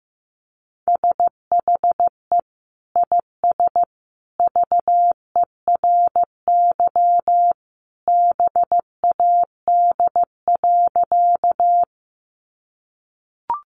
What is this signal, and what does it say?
Morse code, heard as a keyed tone, about 15 wpm, 700 Hz. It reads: SHE IS VERY BAD.
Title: Sentences from Top 700 Words 15wpm